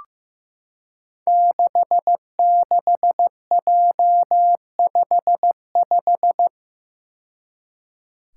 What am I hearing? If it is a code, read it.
Morse code, heard as a keyed tone, about 15 wpm, 700 Hz. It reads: 66J55
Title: Five Challenging Letter-Number Combinations 15wpm